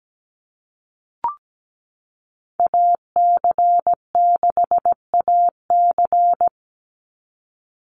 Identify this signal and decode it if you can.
Morse code, heard as a keyed tone, about 17 wpm, 700 Hz. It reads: AC6AC
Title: Call Signs Repeated 3x - 17wpm